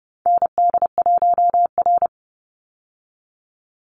Morse code, heard as a keyed tone, about 30 wpm, 700 Hz. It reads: ND1R
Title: Call Signs Repeated 2x - 30wpm